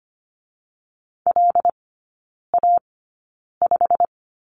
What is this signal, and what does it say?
Morse code, heard as a keyed tone, about 25 wpm, spaced as for 9 wpm, 690 Hz. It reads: LA5
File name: Lesson 15e - Number 5 - Three Characters 25wpm